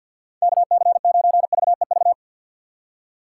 Morse code, heard as a keyed tone, about 49 wpm, 700 Hz. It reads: KC9VEV
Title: USA Call Signs - 6 Characters - 4x Character Spacing - 50wpm